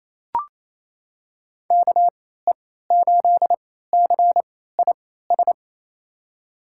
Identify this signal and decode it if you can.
Morse code, heard as a keyed tone, about 28 wpm, spaced as for 16 wpm, 700 Hz. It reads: KE8CIS
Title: USA Call Signs - 6 Characters - 3x Character Spacing - 28wpm